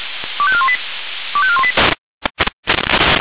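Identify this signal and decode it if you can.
Five-tone selective call (ZVEI), heard as quick runs of five tones, tones of about 70 ms. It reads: 25418 25418
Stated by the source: ZVEI five-tone selective call received on an SDR.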